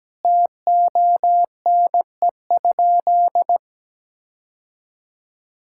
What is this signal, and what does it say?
Morse code, heard as a keyed tone, about 17 wpm, 700 Hz. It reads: TONE?
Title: Sentences from Top 1600 Words 17wpm